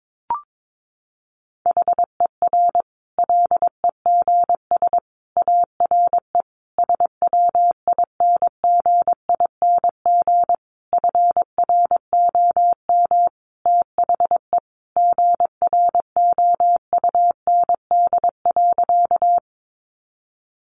Morse code, heard as a keyed tone, about 22 wpm, 700 Hz. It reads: HER LEGS ARE SWINGING FROM THE GROUND.
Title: Easy Phrases 22wpm